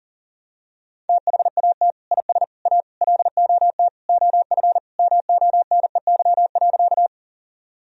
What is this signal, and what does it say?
Morse code, heard as a keyed tone, about 40 wpm, 700 Hz. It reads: THAT IS A LOT OF MONEY.
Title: Sentences from Top 1000 Words - Review Part 3 - 40wpm